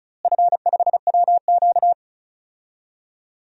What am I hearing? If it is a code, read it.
Morse code, heard as a keyed tone, about 36 wpm, 700 Hz. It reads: F5WQ
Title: DXCC-100 Call Signs - 4 Characters - Repeated 2x - 35wpm